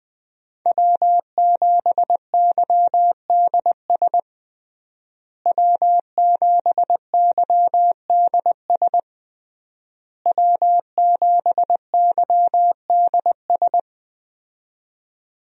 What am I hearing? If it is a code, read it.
Morse code, heard as a keyed tone, about 20 wpm, 700 Hz. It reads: W7YDS W7YDS W7YDS
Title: USA Call Signs - 5 Characters - Repeated 3x - 20wpm